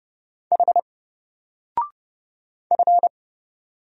Morse code, heard as a keyed tone, about 31 wpm, 700 Hz. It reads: H F
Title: Single Character (Letter:Number:Punctuation:Prosign) 30wpm